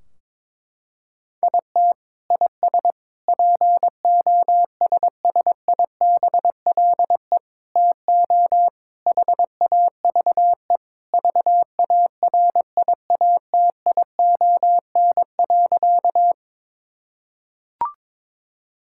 Morse code, heard as a keyed tone, about 22 wpm, 700 Hz. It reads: IT IS POSSIBLE TO HAVE VARIATION.
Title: Sentences from Top 1600 Words 22wpm